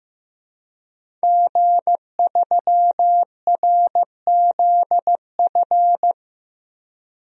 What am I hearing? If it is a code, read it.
Morse code, heard as a keyed tone, about 15 wpm, 700 Hz. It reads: G3RZF
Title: DXCC-100 Call Signs - 5 Characters - Repeated 2x - 15wpm